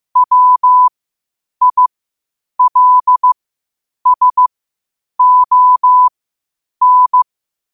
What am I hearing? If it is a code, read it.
Morse code, heard as a keyed tone, about 15 wpm, spaced as for 8 wpm, 1000 Hz. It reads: WILSON
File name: qotdfiles / 8wpm